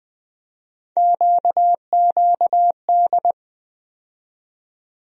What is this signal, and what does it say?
Morse code, heard as a keyed tone, about 20 wpm, 700 Hz. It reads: QQD
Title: USA Call Sign Postfixes - 3 Letters 20wpm